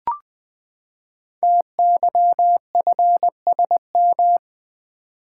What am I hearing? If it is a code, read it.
Morse code, heard as a keyed tone, about 20 wpm, 700 Hz. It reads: TYFSM
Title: Code Groups - 5 Letters - 20wpm